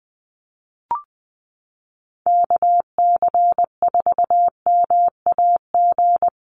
Morse code, heard as a keyed tone, about 20 wpm, 700 Hz. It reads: KC4MAG
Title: USA Call Signs Repeated 3x 20wpm